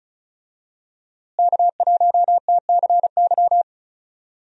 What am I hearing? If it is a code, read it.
Morse code, heard as a keyed tone, about 36 wpm, 700 Hz. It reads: K1TCY
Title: USA Call Signs - 5 Characters - Repeated 3x - 35wpm